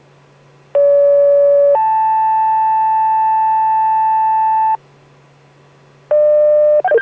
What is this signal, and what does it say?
Two-tone page, heard as a one-second tone, then a three-second tone, about 570 then 880 Hz. It is heard from 0.7 to 4.8 s.